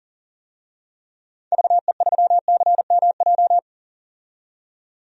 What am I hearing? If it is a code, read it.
Morse code, heard as a keyed tone, about 40 wpm, 700 Hz. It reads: VE3CMJ
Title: DXCC-100 Call Signs - 6 Characters - 40wpm